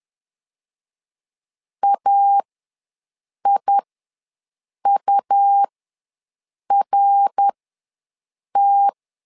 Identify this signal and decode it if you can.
Morse code, heard as a keyed tone, about 11 wpm, spaced as for 6 wpm, 800 Hz. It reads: AIURT
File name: LEZIONI / 3 MWRO / 3 MWR ws 10wpm 2151